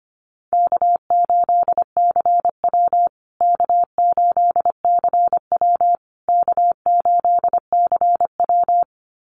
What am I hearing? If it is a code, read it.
Morse code, heard as a keyed tone, about 25 wpm, 700 Hz. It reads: K8CW K8CW K8CW
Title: USA Call Signs Repeated 3x 25wpm